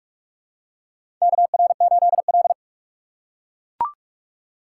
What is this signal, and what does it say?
Morse code, heard as a keyed tone, about 44 wpm, 700 Hz. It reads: KR8L
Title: K1USN SST QSO Practice - 45wpm